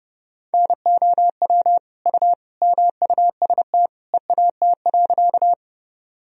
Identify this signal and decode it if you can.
Morse code, heard as a keyed tone, about 32 wpm, 700 Hz. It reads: NOW U MUST EAT.
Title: Sentences from Top 300 Words and Abbreviations 30wpm